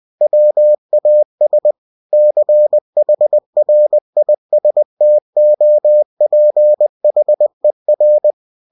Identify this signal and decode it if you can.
Morse code, heard as a keyed tone, about 20 wpm, 600 Hz. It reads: WAS CHRISTOPHER
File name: cw-books / winnie-the-pooh / winnie-20wpm-page-0004